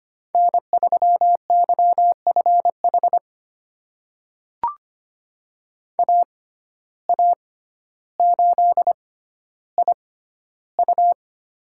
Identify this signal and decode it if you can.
Morse code, heard as a keyed tone, about 25 wpm, spaced as for 13 wpm, 700 Hz. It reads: N3YFH AA8IU
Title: USA Call Signs - 5 Characters - 6x Character Spacing - 25wpm